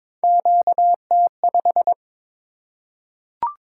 Morse code, heard as a keyed tone, about 22 wpm, 700 Hz. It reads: QT5
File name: practice / 22wpm / Three Letter-Number Combinations - 22wpm